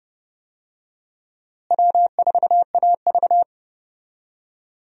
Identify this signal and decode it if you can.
Morse code, heard as a keyed tone, about 29 wpm, 700 Hz. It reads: W4AV
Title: USA Call Signs - 4 Characters - Repeated 3x - 30wpm